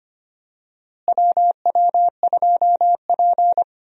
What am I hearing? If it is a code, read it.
Morse code, heard as a keyed tone, about 25 wpm, 700 Hz. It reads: WW2P
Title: USA Call Signs - 4 Characters - Repeated 2x - 25wpm